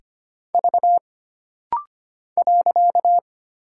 Morse code, heard as a keyed tone, about 25 wpm, 700 Hz. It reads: V .
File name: Single Character (Letter:Number:Punctuation:Prosign) - Rapid-Fire - 25wpm